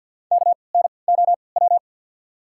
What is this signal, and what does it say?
Morse code, heard as a keyed tone, about 54 wpm, spaced as for 29 wpm, 700 Hz. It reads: KNOW
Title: 3x Character Spacing - Top 300 Words - 50wpm